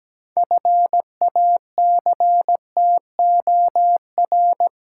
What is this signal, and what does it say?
Morse code, heard as a keyed tone, about 17 wpm, 700 Hz. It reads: FACTOR
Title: Six Letter Words - 17wpm